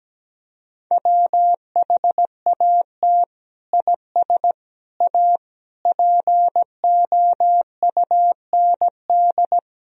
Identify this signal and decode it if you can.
Morse code, heard as a keyed tone, about 17 wpm, 700 Hz. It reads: WHAT IS A POUND
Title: Sentences from Top 1000 Words - Review Part 2 - 17wpm